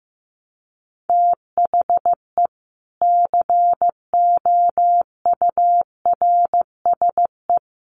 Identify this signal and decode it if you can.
Morse code, heard as a keyed tone, about 15 wpm, 700 Hz. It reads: THE COURSE
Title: Sets of 4 Words 15wpm